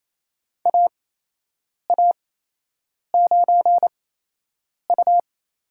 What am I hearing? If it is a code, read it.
Morse code, heard as a keyed tone, about 28 wpm, spaced as for 8 wpm, 700 Hz. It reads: AA9U
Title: USA Call Signs - 5 Characters - 8x Character Spacing - 28wpm